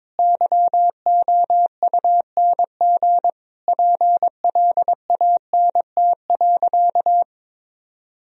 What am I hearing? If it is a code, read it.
Morse code, heard as a keyed tone, about 21 wpm, 700 Hz. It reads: YOUNG PLANT.
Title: Sentences from Top 1000 Words - Review Part 1 - 22wpm